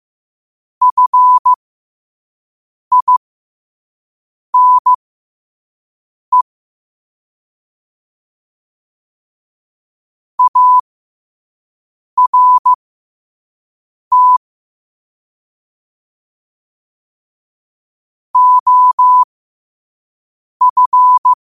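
Morse code, heard as a keyed tone, about 15 wpm, spaced as for 5 wpm, 1000 Hz. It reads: FINE ART OF
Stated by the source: Morse code sent at 5 wpm for learning.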